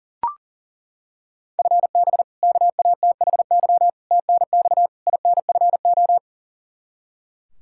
Morse code, heard as a keyed tone, about 40 wpm, 700 Hz. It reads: FB KATHY TNX INFO
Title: Practice QSO - Exchange 3 - More Info - 40wpm